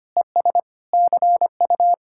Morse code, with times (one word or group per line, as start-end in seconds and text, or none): 0.2-0.6 ES
0.9-1.9 CU